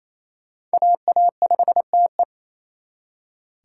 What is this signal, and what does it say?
Morse code, heard as a keyed tone, about 33 wpm, 700 Hz. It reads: AA5TE
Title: Speed Racing - USA 2x2 Call Signs 28wpm